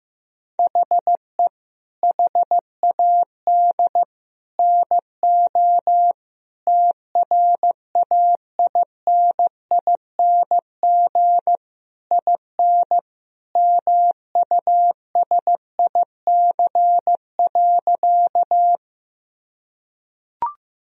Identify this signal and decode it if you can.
Morse code, heard as a keyed tone, about 16 wpm, 700 Hz. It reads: HE HAD NO TRAINING IN MUSIC.
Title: Sentences from Top 1000 Words - Review Part 4 - 15wpm